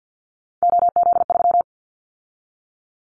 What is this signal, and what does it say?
Morse code, heard as a keyed tone, about 50 wpm, 690 Hz. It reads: OZ3
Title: Callsign Prefixes for Top 100 DXCC - 50wpm